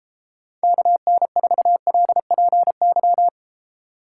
Morse code, heard as a keyed tone, about 30 wpm, 700 Hz. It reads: KN4LPY
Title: Speed Racing - USA Call Signs 28wpm